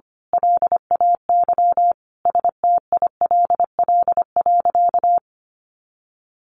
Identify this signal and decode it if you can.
Morse code, heard as a keyed tone, about 25 wpm, 700 Hz. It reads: LAY STILL.
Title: A Sound Of Thunder 25wpm